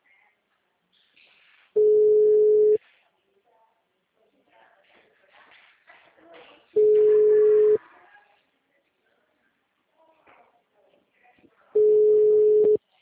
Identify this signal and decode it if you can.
Ringback tone (Europe, 425 Hz), with speech.
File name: out- -2001- - - .3959